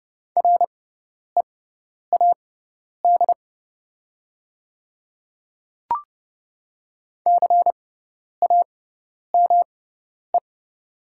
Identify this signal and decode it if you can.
Morse code, heard as a keyed tone, about 29 wpm, spaced as for 10 wpm, 690 Hz. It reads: READ CAME
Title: 6x Character Spacing - Top 200 Words - 30wpm